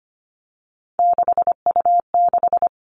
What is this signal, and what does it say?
Morse code, heard as a keyed tone, about 25 wpm, 700 Hz. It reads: 6U6
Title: All Cut Numbers - 3-Digit 25wpm